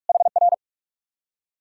Morse code, heard as a keyed tone, about 61 wpm, spaced as for 53 wpm, 700 Hz. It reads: HR
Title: Speed-Racing Common QSO Phrases Encore - 45wpm